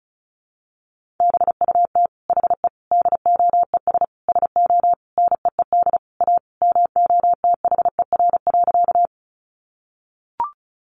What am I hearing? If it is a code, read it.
Morse code, heard as a keyed tone, about 34 wpm, 700 Hz. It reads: BUT HE DOES SO NEED A MOTHER.
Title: Sentences from Top 200 Words 35wpm